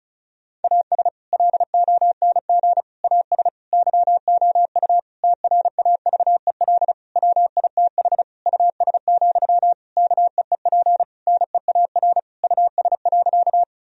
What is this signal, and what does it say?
Morse code, heard as a keyed tone, about 34 wpm, 700 Hz. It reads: AS LONG AS YOU TRAVEL WITH US, KEEP NEAR US.
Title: Sentences from Top 1000 Words - Review Part 2 - 35wpm